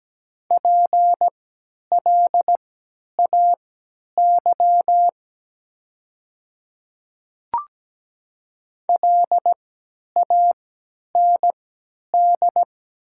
Morse code, heard as a keyed tone, about 17 wpm, spaced as for 10 wpm, 700 Hz. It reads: PLAY LAND
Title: 3x Character Spacing - Top 300 Words - 17wpm